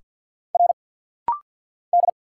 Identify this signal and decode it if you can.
Morse code, heard as a keyed tone, about 50 wpm, 690 Hz. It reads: R D
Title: Single Letters - Rapid-Fire 50wpm